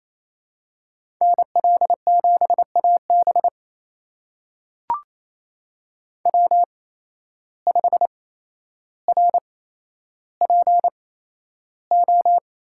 Morse code, heard as a keyed tone, about 28 wpm, spaced as for 12 wpm, 700 Hz. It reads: NL7AB W5RPO
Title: USA Call Signs - 5 Characters - 8x Character Spacing - 28wpm